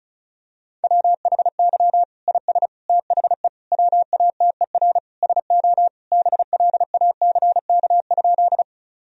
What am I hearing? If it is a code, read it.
Morse code, heard as a keyed tone, about 36 wpm, 700 Hz. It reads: WHY IS THE WATER SO BLACK?
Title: Sentences from Top 1000 Words - Review Part 2 - 35wpm